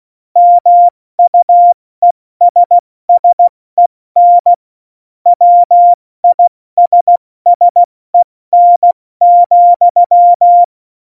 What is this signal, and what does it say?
Morse code, heard as a keyed tone, about 16 wpm, 700 Hz. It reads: MUESSEN WISSEN,
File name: Morsekurs / Witze / Witz 28. Januar 2020, 70 F80